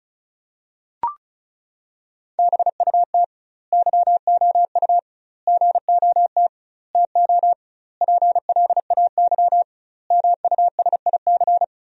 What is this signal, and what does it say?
Morse code, heard as a keyed tone, about 36 wpm, 700 Hz. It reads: BUT YOU GOT TO PLAY MUSIC
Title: 2x Word-Spacing - Sentences from Top 300 Words - 35wpm